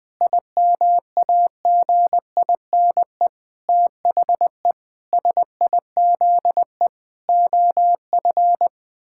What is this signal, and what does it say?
Morse code, heard as a keyed tone, about 21 wpm, 700 Hz. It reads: IMAGINE THE SIZE OF
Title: Sentences with 7-Word Phrases - Spoken Beginning - 20wpm